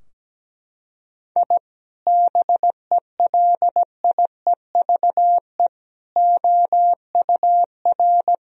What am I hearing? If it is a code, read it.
Morse code, heard as a keyed tone, about 17 wpm, 700 Hz. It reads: I BELIEVE OUR
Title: Sentences from Top 1000 Words - Review Part 3 - 17wpm